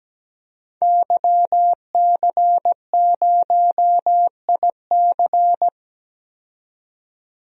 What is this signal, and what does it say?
Morse code, heard as a keyed tone, about 17 wpm, 700 Hz. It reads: YC0IC
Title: DXCC-100 Call Signs - 5 Characters - 17wpm